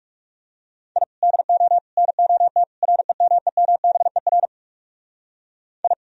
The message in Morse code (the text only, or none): I DO NOT REMEMBER I